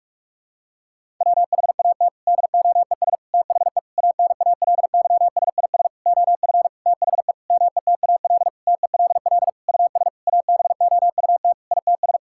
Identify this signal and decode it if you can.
Morse code, heard as a keyed tone, about 47 wpm, 700 Hz. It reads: WHAT DOES THE ANALYSIS OF THE METAL TELL US ABOUT ITS
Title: Sentences from Top 1000 Words - Review Part 6 - 45wpm